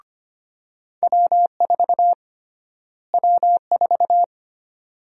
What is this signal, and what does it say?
Morse code, heard as a keyed tone, about 25 wpm, 700 Hz. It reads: W4 W4
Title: USA Call Sign Prefixes - Repeated 2x - 25wpm